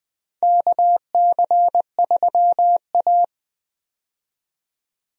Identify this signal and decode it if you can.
Morse code, heard as a keyed tone, about 21 wpm, 700 Hz. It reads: KC3A
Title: USA Call Signs - 4 Characters - 20wpm